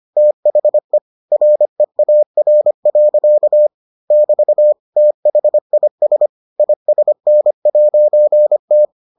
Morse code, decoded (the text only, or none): THE REAR. <BT> THIS ISN'T